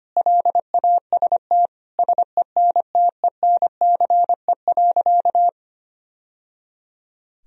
Morse code, heard as a keyed tone, about 25 wpm, 700 Hz. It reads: LAST SENTENCE.
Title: Sentences from Top 1000 Words - Review Part 1 - 25wpm